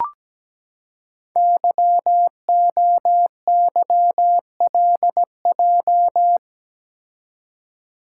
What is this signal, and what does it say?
Morse code, heard as a keyed tone, about 17 wpm, 700 Hz. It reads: YOYLJ
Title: Code Groups - 5 Letter:Numbers - 17wpm